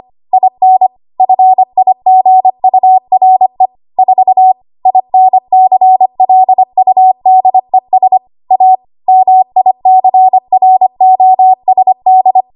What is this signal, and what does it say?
Morse code, heard as a keyed tone, about 25 wpm, 750 Hz. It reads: IN FIGURE 4 INCLUDES A MICROSD
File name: Archive / 25 WPM / 25WPM